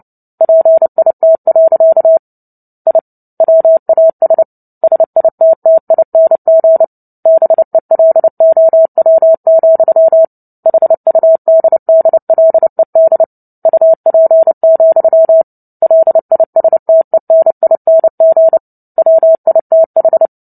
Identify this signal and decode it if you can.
Morse code, heard as a keyed tone, about 30 wpm, 650 Hz. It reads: PIT. I WAS SITTING BELOW, HUDDLED UP, LISTENING WITH
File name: Book Two / 03-Chapter 3 - 29 WPM